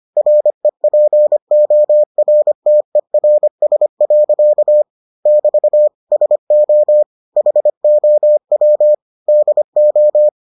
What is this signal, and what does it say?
Morse code, heard as a keyed tone, about 25 wpm, 600 Hz. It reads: REPORTERS. <BT> SO HOW DO